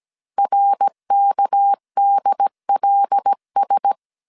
Morse code, heard as a keyed tone, about 17 wpm, 800 Hz. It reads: RKDLS
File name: CW / DET / 1 6 1500C S A 16wpm 2635